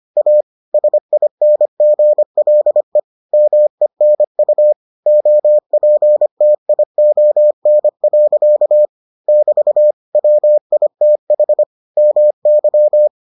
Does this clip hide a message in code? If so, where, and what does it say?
Yes: Morse code from 0.2 to 13.1 s, heard as a keyed tone, about 25 wpm, 600 Hz. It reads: A SINGLE MENU OPTION. <BT> WITH MY